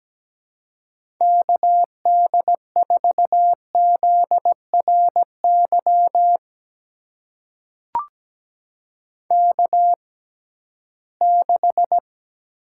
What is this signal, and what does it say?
Morse code, heard as a keyed tone, about 17 wpm, spaced as for 13 wpm, 700 Hz. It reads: KD4ZRY K6
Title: USA Call Signs - 6x Character Spacing - 17wpm